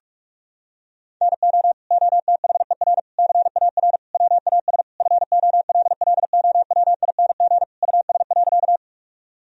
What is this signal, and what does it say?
Morse code, heard as a keyed tone, about 43 wpm, 700 Hz. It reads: NO OTHER CAR WAS FOLLOWING US.
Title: Sentences from Top 1000 Words - Review Part 4 - 45wpm